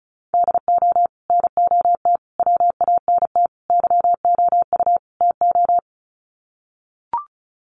Morse code, heard as a keyed tone, about 35 wpm, 700 Hz. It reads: DO NOT WANT YOU TO
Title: Sets of 5 Words 35wpm